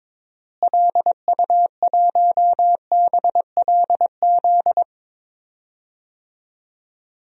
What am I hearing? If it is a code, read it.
Morse code, heard as a keyed tone, about 22 wpm, 700 Hz. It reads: LU1BLZ
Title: DXCC-100 Call Signs - 6 Characters - Repeated 3x - 22wpm